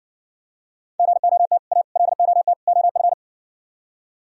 Morse code, heard as a keyed tone, about 55 wpm, 700 Hz. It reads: GOT A LOT OF